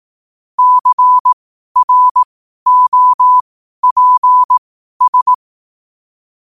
Morse code, heard as a keyed tone, about 18 wpm, spaced as for 12 wpm, 1000 Hz. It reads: CROPS
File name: qotdfiles / 12wpm